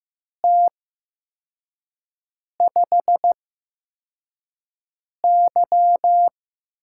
Morse code, heard as a keyed tone, about 15 wpm, 700 Hz. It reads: T 5 Y
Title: USA Call Signs - 8x Character Spacing - 15wpm